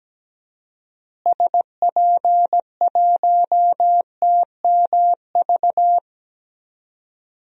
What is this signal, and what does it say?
Morse code, heard as a keyed tone, about 17 wpm, 700 Hz. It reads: SP1TMV
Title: DXCC-100 Call Signs - 6 Characters - Repeated 2x - 17wpm